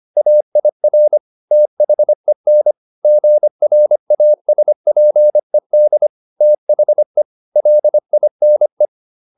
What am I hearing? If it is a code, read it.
Morse code, heard as a keyed tone, about 25 wpm, 600 Hz. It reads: AIR THEN GRASPED THE LINE